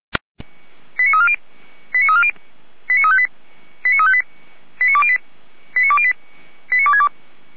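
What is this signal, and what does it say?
Five-tone selective call (ZVEI), heard as quick runs of five tones, tones of about 70 ms. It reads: 89340 89340 89358 89358 89208 89208 89262
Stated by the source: ZVEI five-tone selective call 89303, received by SDR.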